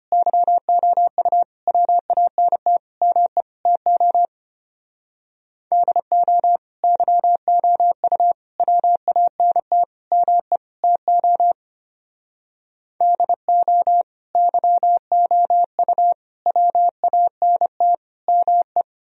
Morse code, heard as a keyed tone, about 30 wpm, 700 Hz. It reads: YOU WANT ME TO DO YOU WANT ME TO DO YOU WANT ME
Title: Speed-Racing - Sets of 5 Words 25wpm